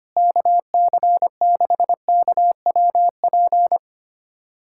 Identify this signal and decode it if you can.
Morse code, heard as a keyed tone, about 25 wpm, 700 Hz. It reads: KC6KWP
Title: USA Callsign DE Callsign - 6 Characters - 25wpm